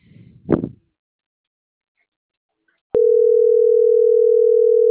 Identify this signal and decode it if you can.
Ringback tone (North America), 440 + 480 Hz.